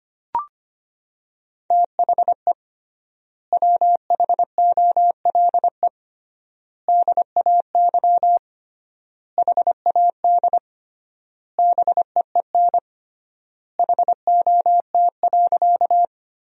Morse code, heard as a keyed tone, about 25 wpm, 700 Hz. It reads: THE WHOLE DAY HAD BEEN HOT.
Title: 3x Word-Spacing - Sentences from Top 500 Words - 25wpm